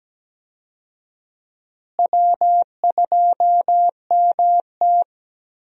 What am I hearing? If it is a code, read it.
Morse code, heard as a keyed tone, about 17 wpm, 700 Hz. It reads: W2MT
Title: USA Call Signs - 4 Characters - Repeated 2x - 17wpm